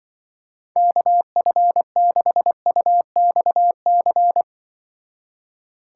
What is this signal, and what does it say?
Morse code, heard as a keyed tone, about 25 wpm, 700 Hz. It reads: KF6UXC
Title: Speed Racing - USA Call Signs 20wpm